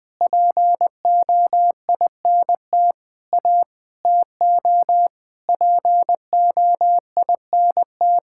Morse code, heard as a keyed tone, about 21 wpm, 700 Hz. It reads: POINT A TO POINT
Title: Sentences with 7-Word Phrases - 20wpm